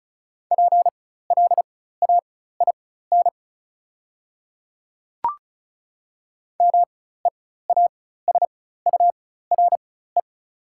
Morse code, heard as a keyed tone, about 35 wpm, spaced as for 16 wpm, 690 Hz. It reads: PLAIN MEASURE
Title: 4x Character Spacing - Top 500 Words - 35wpm